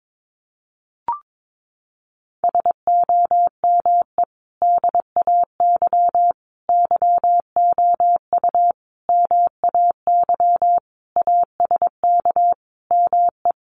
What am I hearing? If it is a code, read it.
Morse code, heard as a keyed tone, about 22 wpm, 700 Hz. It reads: SOME DAY YOU MAY ASK ME
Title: Sentences from Top 200 Words 22wpm